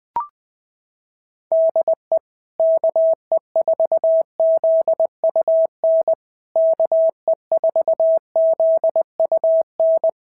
Morse code, heard as a keyed tone, about 20 wpm, 650 Hz. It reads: DE KE4ZUN KE4ZUN
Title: POTA QSO Practice - 20wpm